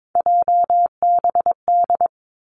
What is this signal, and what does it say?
Morse code, heard as a keyed tone, about 22 wpm, 700 Hz. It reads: JBD
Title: Three Challenging Letter-Number Combinations 22wpm